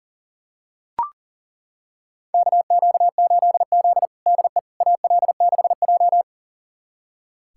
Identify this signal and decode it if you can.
Morse code, heard as a keyed tone, about 37 wpm, 700 Hz. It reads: KQ8Z DE AL6J
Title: USA Callsign DE Callsign - 4 Characters - 40wpm